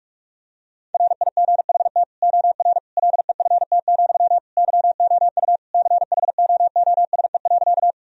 Morse code, heard as a keyed tone, about 46 wpm, 700 Hz. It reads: RIGHT OR LEFT, YOU CHOOSE.
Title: Sentences from Top 1000 Words - Review Part 4 - 45wpm